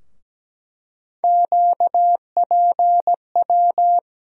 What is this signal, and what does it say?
Morse code, heard as a keyed tone, about 17 wpm, 700 Hz. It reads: QPW
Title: USA Call Sign Postfixes - 3 Letters 17wpm